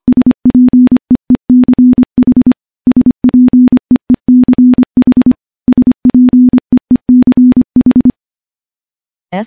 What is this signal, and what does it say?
Morse code, heard as a keyed tone, about 25 wpm, 250 Hz. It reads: SPEECH SPEECH SPEECH
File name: morse-training-25WPM